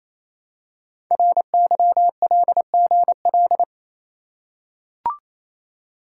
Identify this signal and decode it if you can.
Morse code, heard as a keyed tone, about 27 wpm, 700 Hz. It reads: RYLGL
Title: Code Groups - 5 Letters - 28wpm